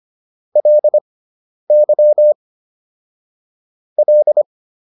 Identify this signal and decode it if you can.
Morse code, heard as a keyed tone, about 25 wpm, spaced as for 10 wpm, 600 Hz. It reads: LY L